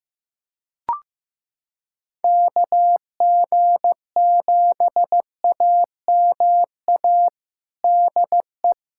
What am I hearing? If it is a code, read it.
Morse code, heard as a keyed tone, about 15 wpm, 700 Hz. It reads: KG7AMA DE
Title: USA Callsign DE Callsign 15wpm